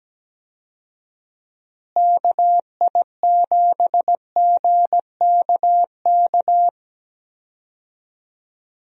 Morse code, heard as a keyed tone, about 17 wpm, 700 Hz. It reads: KI7GKK
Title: USA Call Signs - 6 Characters - Repeated 3x - 17wpm